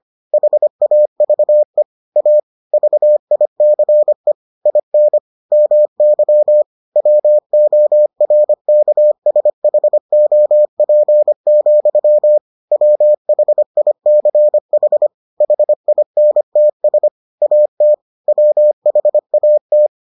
Morse code, heard as a keyed tone, about 25 wpm, 600 Hz. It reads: HAVE A VICE IN MY WORKSHOP, WHICH HINTS AT WHAT